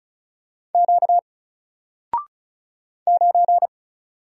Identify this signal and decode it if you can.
Morse code, heard as a keyed tone, about 34 wpm, 700 Hz. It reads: Q 9
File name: Single-Letter-Number-35wpm